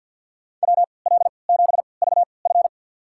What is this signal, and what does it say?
Morse code, heard as a keyed tone, about 46 wpm, spaced as for 27 wpm, 700 Hz. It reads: WL7VF
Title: USA Call Signs - 5 Characters - 3x Character Spacing - 50wpm